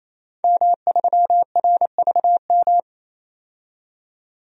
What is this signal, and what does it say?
Morse code, heard as a keyed tone, about 28 wpm, 700 Hz. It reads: M3RVM
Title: DXCC-100 Call Signs - 5 Characters - 28wpm